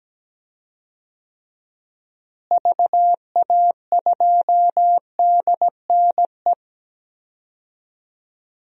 Morse code, heard as a keyed tone, about 17 wpm, 700 Hz. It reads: VA2DNE
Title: DXCC-100 Call Signs - 6 Characters - Repeated 2x - 17wpm